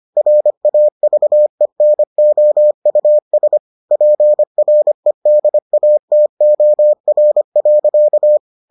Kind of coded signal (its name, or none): Morse code